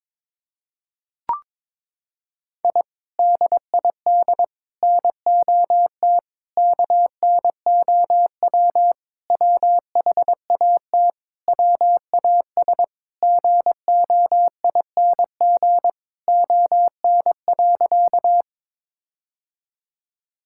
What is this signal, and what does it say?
Morse code, heard as a keyed tone, about 22 wpm, 700 Hz. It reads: I DID NOT KNOW WHAT WAS GOING ON.
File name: practice / 22wpm / Sentences with 4-Word Phrases - 22wpm